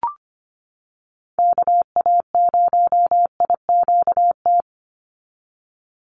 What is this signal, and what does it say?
Morse code, heard as a keyed tone, about 25 wpm, 700 Hz. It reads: KA0IQT
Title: Call Signs - 25wpm